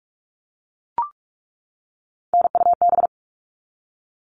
Morse code, heard as a keyed tone, about 44 wpm, 700 Hz. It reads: NUB
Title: All Cut Numbers - 3-Digit 45wpm